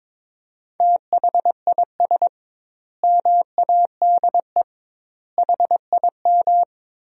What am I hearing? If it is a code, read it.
Morse code, heard as a keyed tone, about 22 wpm, 700 Hz. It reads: THIS MADE HIM
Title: 2x Word-Spacing - Sentences from Top 500 Words - 22wpm